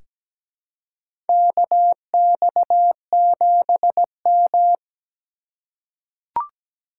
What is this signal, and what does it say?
Morse code, heard as a keyed tone, about 17 wpm, 700 Hz. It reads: KX7M
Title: K1USN SST QSO Practice - 17wpm